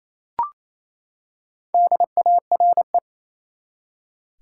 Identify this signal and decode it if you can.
Morse code, heard as a keyed tone, about 28 wpm, 700 Hz. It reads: DARE
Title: Four Letter Words - 28wpm